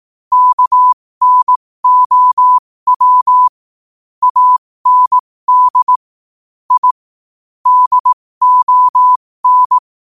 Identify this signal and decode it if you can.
Morse code, heard as a keyed tone, about 18 wpm, spaced as for 15 wpm, 1000 Hz. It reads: KNOW AND I DON
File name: qotdfiles / 15wpm